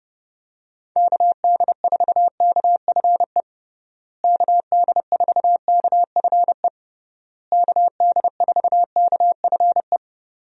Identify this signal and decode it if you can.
Morse code, heard as a keyed tone, about 30 wpm, 700 Hz. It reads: KD4KFE KD4KFE KD4KFE
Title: USA Call Signs - 6 Characters - Repeated 3x - 30wpm